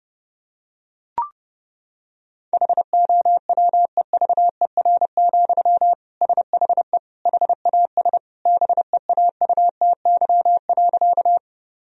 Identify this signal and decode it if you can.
Morse code, heard as a keyed tone, about 30 wpm, 700 Hz. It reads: HOWEVER, SHE HAS BEAUTY.
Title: Sentences from Top 1000 Words - Review Part 2 - 30wpm